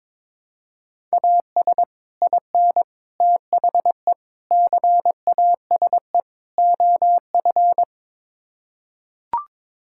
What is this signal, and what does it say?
Morse code, heard as a keyed tone, about 22 wpm, 700 Hz. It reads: AS IN THE CASE OF
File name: practice / 22wpm / Build A Phrase - 5-Word Phrases - 22wpm